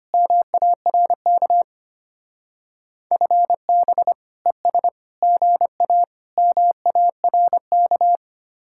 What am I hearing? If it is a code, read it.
Morse code, heard as a keyed tone, about 31 wpm, 700 Hz. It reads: MARK FB ES GA MARK
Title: Speed-Racing Common QSO Phrases Encore - 25wpm